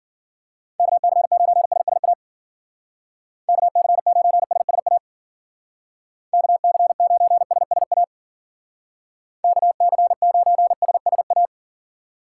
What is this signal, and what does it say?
Morse code, heard as a keyed tone, about 49 wpm, 700 Hz. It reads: KC9SSA KC9SSA KC9SSA KC9SSA
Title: Speed Racing - USA Call Signs 40wpm